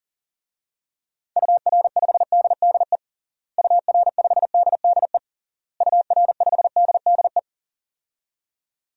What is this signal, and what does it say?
Morse code, heard as a keyed tone, about 40 wpm, 700 Hz. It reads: UR5DDE UR5DDE UR5DDE
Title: DXCC-100 Call Signs - 6 Characters - Repeated 3x - 40wpm